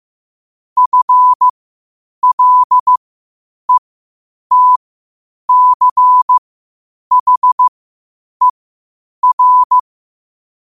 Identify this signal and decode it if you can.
Morse code, heard as a keyed tone, about 15 wpm, spaced as for 8 wpm, 1000 Hz. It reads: FLETCHER
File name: qotdfiles / 8wpm